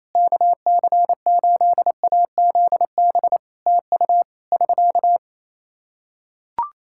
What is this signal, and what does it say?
Morse code, heard as a keyed tone, about 28 wpm, 700 Hz. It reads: KC8AZB TU <SK>
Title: Practice QSO - Exchange 4 - Ending - 28wpm